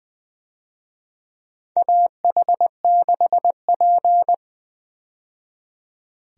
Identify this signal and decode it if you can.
Morse code, heard as a keyed tone, about 20 wpm, 700 Hz. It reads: AH6P
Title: USA Call Signs - 4 Characters - 20wpm